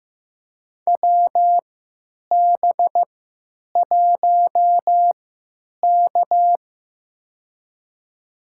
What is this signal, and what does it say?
Morse code, heard as a keyed tone, about 15 wpm, spaced as for 9 wpm, 700 Hz. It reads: WB1K
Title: USA Call Signs - 4 Characters - 3x Character Spacing - 15wpm